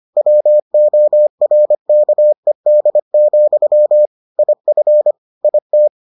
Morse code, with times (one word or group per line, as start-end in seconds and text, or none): 0.2-4.1 WORKED,
4.4-5.1 IF
5.4-5.9 IT